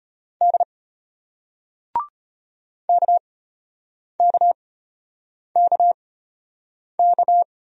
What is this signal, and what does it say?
Morse code, heard as a keyed tone, about 36 wpm, spaced as for 8 wpm, 700 Hz. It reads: D KKKK